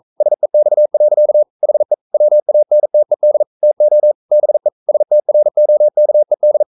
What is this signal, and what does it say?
Morse code, heard as a keyed tone, about 42 wpm, 600 Hz. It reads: SEX. HE WANTED TO BE STROKED
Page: Romanen Never af Ken Follet på engelsk.